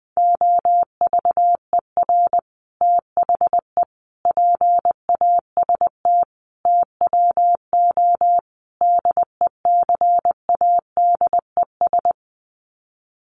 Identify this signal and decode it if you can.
Morse code, heard as a keyed tone, about 20 wpm, 700 Hz. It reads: OVER THE PAST TWO DECADES
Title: Sets of 5 Words Encore2 20wpm